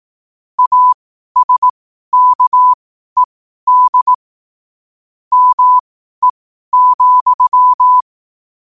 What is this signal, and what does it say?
Morse code, heard as a keyed tone, about 18 wpm, spaced as for 12 wpm, 1000 Hz. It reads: ASKED ME,
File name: qotdfiles / 12wpm